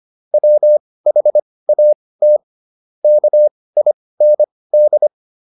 Morse code, heard as a keyed tone, about 25 wpm, spaced as for 18 wpm, 600 Hz. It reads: WHAT KIND